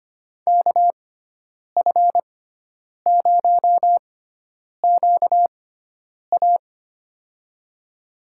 Morse code, heard as a keyed tone, about 25 wpm, spaced as for 9 wpm, 700 Hz. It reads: KF0QA
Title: USA Call Signs - 6x Character Spacing - 25wpm